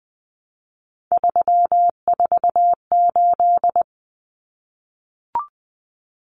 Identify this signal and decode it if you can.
Morse code, heard as a keyed tone, about 20 wpm, 700 Hz. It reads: 348
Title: Common Cut Numbers - 3-Digit 20wpm